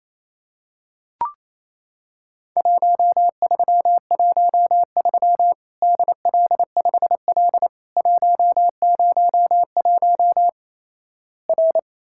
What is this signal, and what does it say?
Morse code, heard as a keyed tone, about 28 wpm, 700 Hz. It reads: 1313 DL5L 101 R
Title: Worked All Europe DX Contest - QTC Exchange - 28wpm